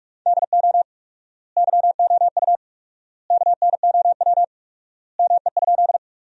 Morse code, heard as a keyed tone, about 41 wpm, 700 Hz. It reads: DO YOU KNOW ME?
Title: 4x Word-Spacing - Sentences from Top 200 Words - 45wpm